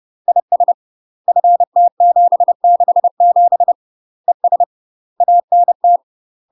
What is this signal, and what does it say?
Morse code, heard as a keyed tone, about 30 wpm, 700 Hz. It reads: IS FT767 ES ANT
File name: UK MASTER QSOs TRAINING / 30WPM / UK MASTER QSOs 0046